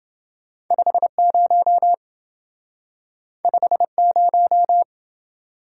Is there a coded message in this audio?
Yes: Morse code from 0.7 to 4.8 s, heard as a keyed tone, about 27 wpm, 700 Hz. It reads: 50 50